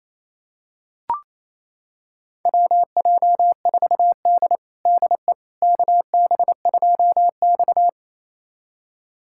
Morse code, heard as a keyed tone, about 28 wpm, 700 Hz. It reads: WJ4D DE KB2X
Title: USA Callsign DE Callsign - 4 Characters - 28wpm